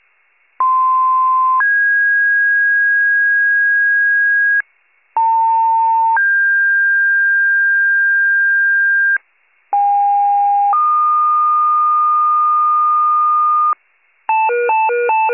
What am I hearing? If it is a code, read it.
Two-tone page, heard as a one-second tone, then a three-second tone, about 1040 then 1650 Hz, about 900 then 1600 Hz, about 810 then 1220 Hz. It is heard from 0.6 to 13.7 s.
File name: pager / 78 RESCUE DUTY / wcalert